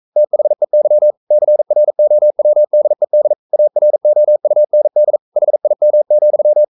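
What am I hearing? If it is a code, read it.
Morse code, heard as a keyed tone, about 42 wpm, 600 Hz. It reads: THEY CROWDED AROUND HIM,
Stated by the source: Morse code practice at 41 wpm, from CW med Gnister.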